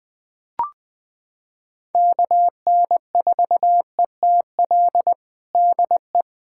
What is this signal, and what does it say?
Morse code, heard as a keyed tone, about 20 wpm, 700 Hz. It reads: KN4ETL DE
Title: USA Callsign DE Callsign 20wpm